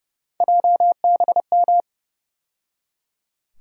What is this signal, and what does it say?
Morse code, heard as a keyed tone, about 30 wpm, 700 Hz. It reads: JBM
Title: USA Call Sign Postfixes - 3 Letters 30wpm